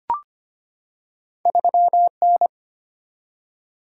Morse code, heard as a keyed tone, about 25 wpm, 700 Hz. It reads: 3N
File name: Two Letter-Number Combinations - 25wpm